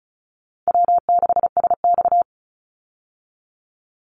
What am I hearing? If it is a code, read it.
Morse code, heard as a keyed tone, about 34 wpm, 700 Hz. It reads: W6SX
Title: Call Signs Repeated 3x - 35wpm